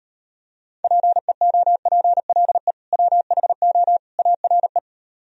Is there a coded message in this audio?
Yes: Morse code from 0.8 to 4.8 s, heard as a keyed tone, about 32 wpm, 700 Hz. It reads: PEOPLE WHO ARE